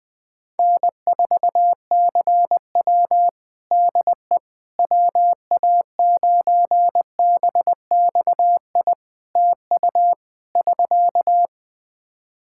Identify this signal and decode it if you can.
Morse code, heard as a keyed tone, about 20 wpm, 700 Hz. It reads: N4CW DE WA9BXI TU <SK>
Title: Practice QSO - Exchange 4 - Ending - 20wpm